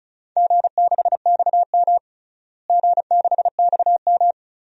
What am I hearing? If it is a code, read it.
Morse code, heard as a keyed tone, about 34 wpm, 700 Hz. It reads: G6XM G6XM
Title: DXCC-100 Call Signs - 4 Characters - Repeated 3x - 35wpm